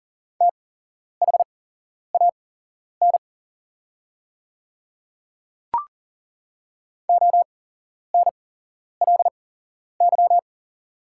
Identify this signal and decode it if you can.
Morse code, heard as a keyed tone, about 39 wpm, spaced as for 11 wpm, 700 Hz. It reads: THAN ONLY
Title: 8x Character Spacing - Top 200 Words - 40wpm